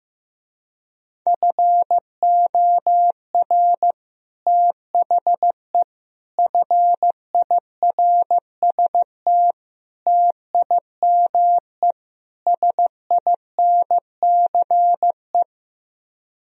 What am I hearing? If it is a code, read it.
Morse code, heard as a keyed tone, about 15 wpm, 700 Hz. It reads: FOR THE FIRST TIME SINCE
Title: Speed-Racing - Sets of 5 Words 15wpm